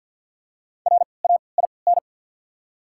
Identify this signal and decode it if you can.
Morse code, heard as a keyed tone, about 54 wpm, spaced as for 29 wpm, 690 Hz. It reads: RAIN
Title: 3x Character Spacing - Top 300 Words - 50wpm